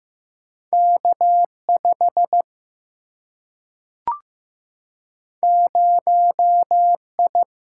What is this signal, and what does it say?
Morse code, heard as a keyed tone, about 15 wpm, 700 Hz. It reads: K 5 0 I
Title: Single-Letter Single-Number 15wpm